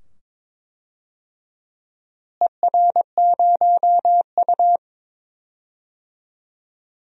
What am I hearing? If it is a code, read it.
Morse code, heard as a keyed tone, about 22 wpm, 700 Hz. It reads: ER0U
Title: DXCC-100 Call Signs - 4 Characters - Repeated 2x - 22wpm